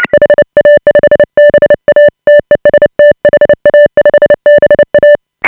Morse code, heard as a keyed tone, about 29 wpm, 590 Hz. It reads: HA5BA TEST HA5BA
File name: AUDIO2018 / HA5BA CW